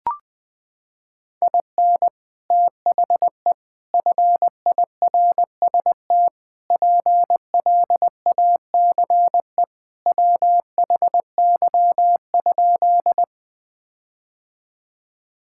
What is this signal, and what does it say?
Morse code, heard as a keyed tone, about 20 wpm, 700 Hz. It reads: IN THE FIRST PLACE WHY?
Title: Sentences with 4-Word Phrases - 20wpm